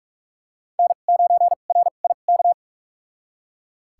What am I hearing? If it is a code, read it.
Morse code, heard as a keyed tone, about 41 wpm, spaced as for 31 wpm, 700 Hz. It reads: N9RIK
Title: USA Call Signs - 5 Characters - 2x Character Spacing - 45wpm